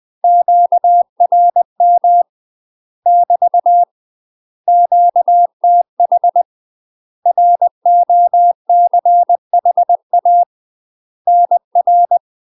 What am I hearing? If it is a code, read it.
Morse code, heard as a keyed tone, about 20 wpm, 700 Hz. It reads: QRM <BT> QTH ROCHA NR
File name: UK MASTER QSOs TRAINING / 20WPM / UK MASTER QSOs 0032